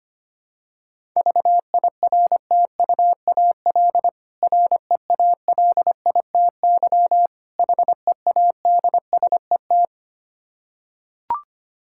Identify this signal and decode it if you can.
Morse code, heard as a keyed tone, about 25 wpm, 700 Hz. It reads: VIRTUAL REALITY HEADSET
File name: practice / 25wpm / Difficult Words with Hints - 25wpm